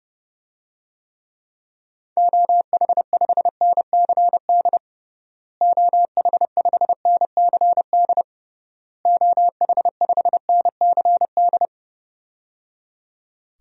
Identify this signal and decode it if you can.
Morse code, heard as a keyed tone, about 30 wpm, 700 Hz. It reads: OH5NCD OH5NCD OH5NCD
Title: DXCC-100 Call Signs - 6 Characters - Repeated 3x - 30wpm